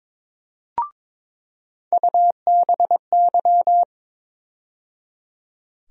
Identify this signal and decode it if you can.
Morse code, heard as a keyed tone, about 22 wpm, 700 Hz. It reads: UBY
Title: USA Call Sign Postfixes - 3 Letters 22wpm